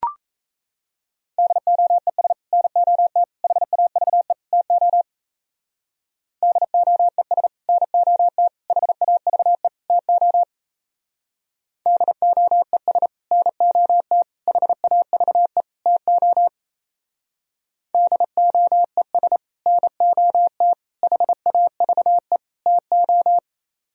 Morse code, heard as a keyed tone, about 35 wpm, 700 Hz. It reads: DOES NOT HAVE TO DOES NOT HAVE TO DOES NOT HAVE TO DOES NOT HAVE TO
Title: Speed-Racing - Sets of 4 Words 28wpm